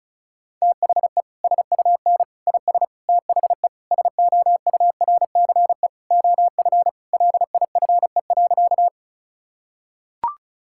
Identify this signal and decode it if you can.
Morse code, heard as a keyed tone, about 36 wpm, 690 Hz. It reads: THE SUN IS THE SOURCE OF LIFE.
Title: Sentences from Top 1000 Words - Review Part 5 - 35wpm